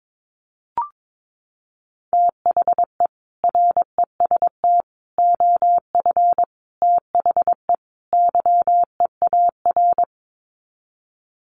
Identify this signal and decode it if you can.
Morse code, heard as a keyed tone, about 22 wpm, 700 Hz. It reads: THE REST OF THE YEAR
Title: Sets of 5 Words Encore 22wpm